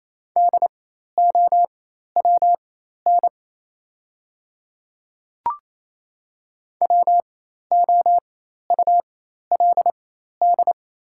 Morse code, heard as a keyed tone, about 28 wpm, spaced as for 13 wpm, 700 Hz. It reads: DOWN WOULD
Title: 4x Character Spacing - Top 200 Words - 28wpm